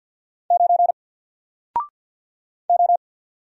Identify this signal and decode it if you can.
Morse code, heard as a keyed tone, about 48 wpm, 700 Hz. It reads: 9 O
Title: Single Letter-Number Alt 50wpm